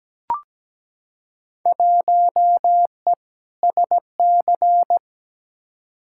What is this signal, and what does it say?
Morse code, heard as a keyed tone, about 17 wpm, 700 Hz. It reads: 1E SC
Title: Field Day Exchanges - 17wpm